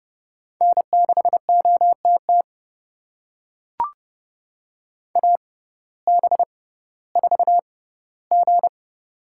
Morse code, heard as a keyed tone, about 29 wpm, spaced as for 16 wpm, 700 Hz. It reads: N6OTT AB4G
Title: USA Call Signs - 5 Characters - 6x Character Spacing - 30wpm